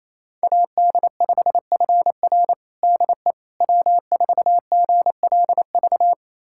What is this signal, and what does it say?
Morse code, heard as a keyed tone, about 28 wpm, 700 Hz. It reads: AD5FR DE W4GLV
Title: USA Callsign DE Callsign - 5 Characters - 28wpm